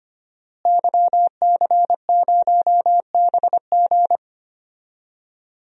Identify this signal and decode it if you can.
Morse code, heard as a keyed tone, about 25 wpm, 700 Hz. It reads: YC0BG
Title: DXCC-100 Call Signs - 5 Characters - Repeated 2x - 25wpm